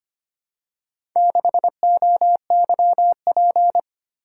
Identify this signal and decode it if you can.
Morse code, heard as a keyed tone, about 25 wpm, 700 Hz. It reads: 6OYP
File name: practice / 25wpm / Four Letter-Number Combinations - 25wpm